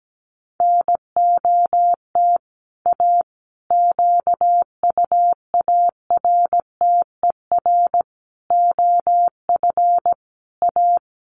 Morse code, heard as a keyed tone, about 17 wpm, 700 Hz. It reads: NOT A QUARTER OF A
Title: Sentences from Top 1300 Words 17wpm